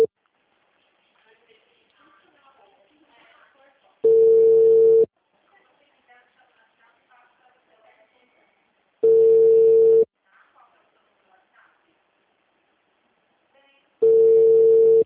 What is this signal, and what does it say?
Ringback tone (China, 450 Hz), with speech.
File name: out- -2011- - - .15143